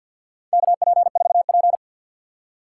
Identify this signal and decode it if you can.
Morse code, heard as a keyed tone, about 52 wpm, 700 Hz. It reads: KP4P
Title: USA Call Signs - 4 Characters - 2x Character Spacing - 50wpm